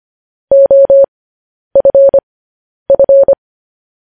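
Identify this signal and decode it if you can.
Morse code, heard as a keyed tone, about 25 wpm, spaced as for 10 wpm, 550 Hz. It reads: OFF
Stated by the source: News headlines sent in Morse at 10 wpm.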